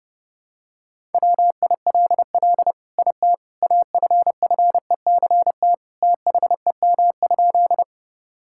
Morse code, heard as a keyed tone, about 30 wpm, 700 Hz. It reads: WILL IT AFFECT THEM?
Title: Sentences from Top 1000 Words - Review Part 6 - 30wpm